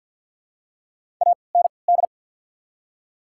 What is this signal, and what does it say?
Morse code, heard as a keyed tone, about 48 wpm, spaced as for 28 wpm, 700 Hz. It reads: AND
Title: 3x Character Spacing - Top 100 Words - 50wpm